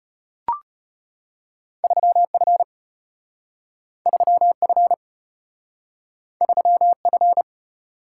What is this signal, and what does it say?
Morse code, heard as a keyed tone, about 34 wpm, 700 Hz. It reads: 3F 3F 3F